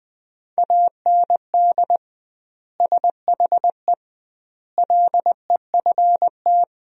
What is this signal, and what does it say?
Morse code, heard as a keyed tone, about 19 wpm, 700 Hz. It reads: AND SHE LEFT
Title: 2x Word-Spacing - Sentences from Top 300 Words - 20wpm